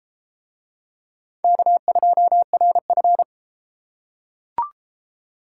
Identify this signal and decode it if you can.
Morse code, heard as a keyed tone, about 31 wpm, 700 Hz. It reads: K2RF